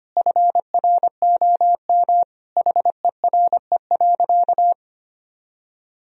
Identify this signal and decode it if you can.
Morse code, heard as a keyed tone, about 25 wpm, 700 Hz. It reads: FROM HERE.
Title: Sentences from Top 1000 Words - Review Part 1 - 25wpm